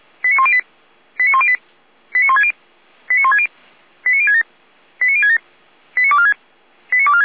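Five-tone selective call (ZVEI), heard as quick runs of five tones, tones of about 70 ms. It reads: 89108 89108 89160 89160 89067 89067 89246 89246
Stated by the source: ZVEI five-tone call received by SDR.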